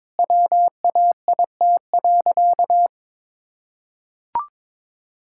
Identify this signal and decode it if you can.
Morse code, heard as a keyed tone, about 22 wpm, 700 Hz. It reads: WAIT.
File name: practice / 22wpm / Sentences with 3-Word Phrases - 22wpm